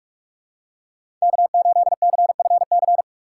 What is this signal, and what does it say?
Morse code, heard as a keyed tone, about 46 wpm, 700 Hz. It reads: K8CFC
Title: USA Call Signs - 6x Character Spacing - 45wpm